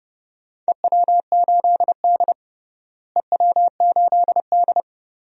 Morse code, heard as a keyed tone, about 31 wpm, 700 Hz. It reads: EW8D EW8D
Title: DXCC-100 Call Signs - 4 Characters - Repeated 2x - 30wpm